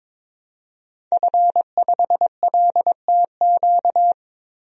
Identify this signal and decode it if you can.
Morse code, heard as a keyed tone, about 22 wpm, 700 Hz. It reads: F5LTQ
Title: DXCC-100 Call Signs - 5 Characters - 22wpm